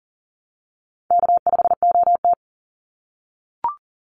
Morse code, heard as a keyed tone, about 40 wpm, 700 Hz. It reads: K5OT
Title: NA SOTA Chaser Callsigns 40wpm